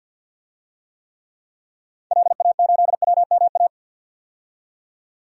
Morse code, heard as a keyed tone, about 46 wpm, 700 Hz. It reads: LA8WMA
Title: DXCC-100 Call Signs - 6 Characters - 50wpm